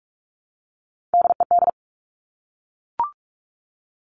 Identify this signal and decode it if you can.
Morse code, heard as a keyed tone, about 46 wpm, 700 Hz. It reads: DED
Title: Top Three Letter Combinations in Words 45wpm